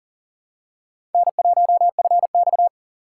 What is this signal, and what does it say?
Morse code, heard as a keyed tone, about 40 wpm, 700 Hz. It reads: N1FX
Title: USA Call Signs - 4 Characters - 3x Character Spacing - 40wpm